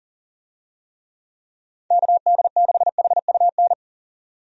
Morse code, heard as a keyed tone, about 40 wpm, 700 Hz. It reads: KD6HUN
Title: USA Call Signs - 6 Characters - 40wpm